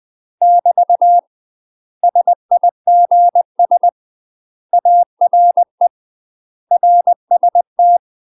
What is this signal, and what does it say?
Morse code, heard as a keyed tone, about 20 wpm, 700 Hz. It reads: <BT> SIGS ARE RST
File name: UK MASTER QSOs TRAINING / 20WPM / UK MASTER QSOs 0043